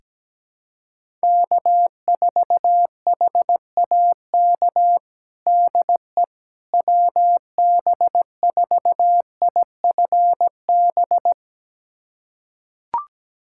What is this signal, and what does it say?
Morse code, heard as a keyed tone, about 17 wpm, 700 Hz. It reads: K4HAK DE WB4IFB
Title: USA Callsign DE Callsign 17wpm